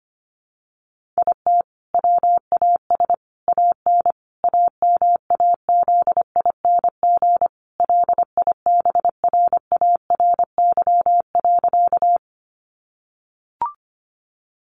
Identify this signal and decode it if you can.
Morse code, heard as a keyed tone, about 25 wpm, 700 Hz. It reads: IT WAS AN AMAZING LIBRARY.
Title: Sentences from Top 1500 Words 25wpm